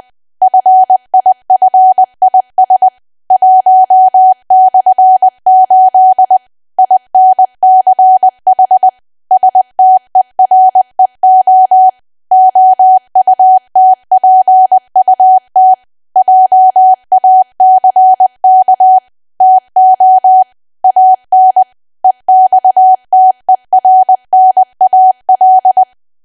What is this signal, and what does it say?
Morse code, heard as a keyed tone, about 20 wpm, 750 Hz. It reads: FIFIS 1/8 INCH STEREO OUTPUT JACK TO AN EXTERNAL
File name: cwa / W1AW-Level3 / 20WPM